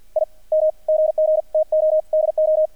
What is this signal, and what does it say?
Morse code, heard as a keyed tone, about 48 wpm, spaced as for 37 wpm, 620 Hz, with radio noise. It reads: IM GG TO DO
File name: archived / CRSnet